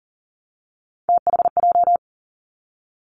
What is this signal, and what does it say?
Morse code, heard as a keyed tone, about 42 wpm, 700 Hz. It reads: THJ
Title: Three Challenging Letter-Number Combinations 40wpm